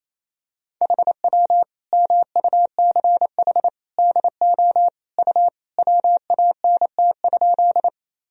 Morse code, heard as a keyed tone, about 28 wpm, 700 Hz. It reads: HW MUCH DO U WANT?
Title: Sentences from Top 200 Words and Abbreviations 28wpm